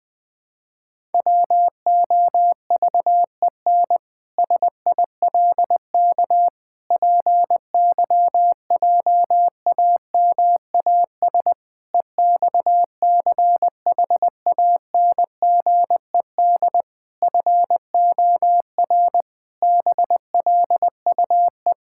Morse code, heard as a keyed tone, about 20 wpm, 700 Hz. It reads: WOVEN SILK PYJAMAS EXCHANGED FOR BLUE
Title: Pangram Challenge v2 - 20wpm